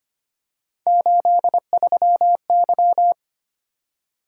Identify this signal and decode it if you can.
Morse code, heard as a keyed tone, about 29 wpm, 700 Hz. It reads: 83Y
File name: Speed-Racing - Three Letter-Number Combinations - 25wpm